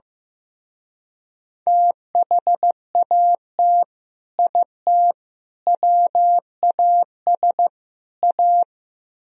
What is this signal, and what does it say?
Morse code, heard as a keyed tone, about 15 wpm, 700 Hz. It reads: THAT IT WAS A
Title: Speed-Racing - Sets of 4 Words 15wpm